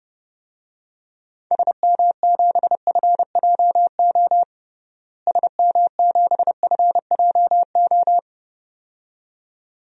Morse code, heard as a keyed tone, about 30 wpm, 700 Hz. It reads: SM7FJO SM7FJO
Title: DXCC-100 Call Signs - 6 Characters - Repeated 2x - 30wpm